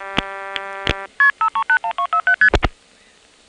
Multi-frequency trunk signalling, heard as fast pairs of tones, tones of about 90 ms. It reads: KP5312ST